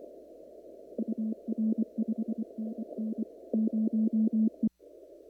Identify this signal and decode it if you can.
Morse code, heard as a keyed tone, about 24 wpm, 220 Hz. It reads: UR 5NN 0E
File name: CW / kt5j wpx cw-op-k5pi